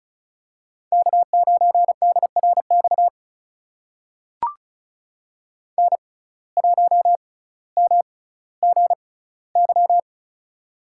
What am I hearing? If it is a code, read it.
Morse code, heard as a keyed tone, about 36 wpm, spaced as for 18 wpm, 700 Hz. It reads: K9DRX N1MGY
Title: USA Call Signs - 5 Characters - 6x Character Spacing - 35wpm